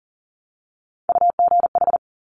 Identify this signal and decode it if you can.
Morse code, heard as a keyed tone, about 41 wpm, 690 Hz. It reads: UGH
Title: Top Three Letter Combinations in Words 40wpm